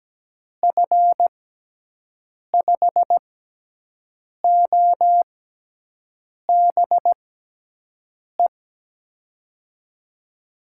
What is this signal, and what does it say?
Morse code, heard as a keyed tone, about 17 wpm, spaced as for 6 wpm, 700 Hz. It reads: F5OBE
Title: USA Call Signs - 6 Characters - 6x Character Spacing - 17wpm